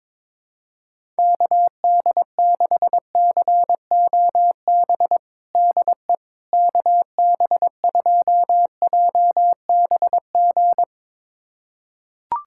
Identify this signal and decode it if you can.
Morse code, heard as a keyed tone, about 22 wpm, 700 Hz. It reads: KD6COB DE KB2JBG
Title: USA Callsign DE Callsign - 6 Characters - 22wpm